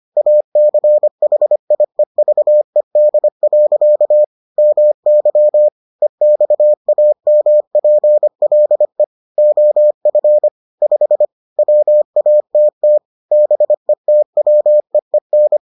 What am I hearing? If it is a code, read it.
Morse code, heard as a keyed tone, about 25 wpm, 600 Hz. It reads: ACHIEVED. MY EXAMPLE OF 5 WATT BETWEEN